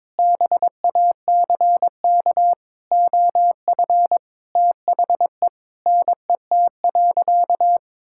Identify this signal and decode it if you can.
Morse code, heard as a keyed tone, about 22 wpm, 700 Hz. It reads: BACK OF THE NET.
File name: practice / 22wpm / Sentences with 8-Word Phrases - 22wpm